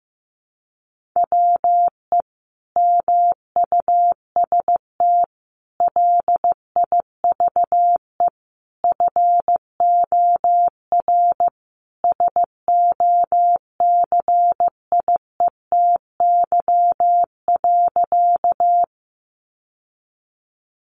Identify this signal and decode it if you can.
Morse code, heard as a keyed tone, about 15 wpm, 700 Hz. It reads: WE MUST LIVE FOR SOCIETY.
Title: Sentences from Top 700 Words 15wpm